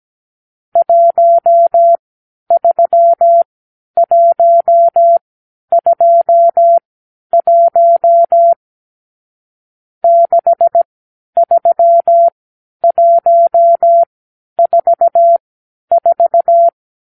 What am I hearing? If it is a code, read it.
Morse code, heard as a keyed tone, about 17 wpm, spaced as for 10 wpm, 680 Hz. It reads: 13121 63144
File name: k7qo / Lesson 52 (8)